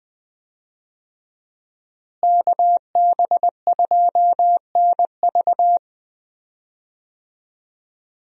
Morse code, heard as a keyed tone, about 20 wpm, 700 Hz. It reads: KB2NV
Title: USA Call Signs - 5 Characters - Repeated 2x - 20wpm